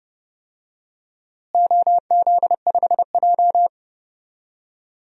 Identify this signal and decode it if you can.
Morse code, heard as a keyed tone, about 30 wpm, 700 Hz. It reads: OZ5J
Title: DXCC-100 Call Signs - 4 Characters - Repeated 2x - 30wpm